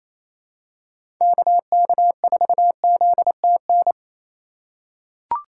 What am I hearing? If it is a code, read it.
Morse code, heard as a keyed tone, about 28 wpm, 700 Hz. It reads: KK4ZTN
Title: USA Call Signs - 6 Characters - 4x Character Spacing - 28wpm